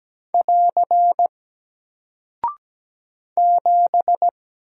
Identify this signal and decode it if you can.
Morse code, heard as a keyed tone, about 17 wpm, 700 Hz. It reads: <AR> 7
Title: Single Character (Letter:Number:Punctuation:Prosign) Alt 17wpm